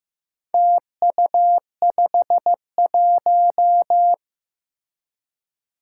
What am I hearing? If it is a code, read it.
Morse code, heard as a keyed tone, about 15 wpm, 700 Hz. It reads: TU51
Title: Four Challenging Letter-Number Combinations 15wpm